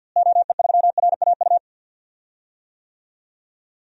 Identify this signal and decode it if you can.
Morse code, heard as a keyed tone, about 46 wpm, 700 Hz. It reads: OE3RAU
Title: DXCC-100 Call Signs - 6 Characters - Repeated 3x - 50wpm